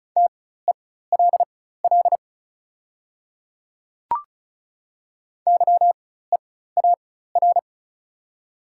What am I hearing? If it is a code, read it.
Morse code, heard as a keyed tone, about 36 wpm, spaced as for 16 wpm, 690 Hz. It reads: TELL YEAR
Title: 4x Character Spacing - Top 300 Words - 35wpm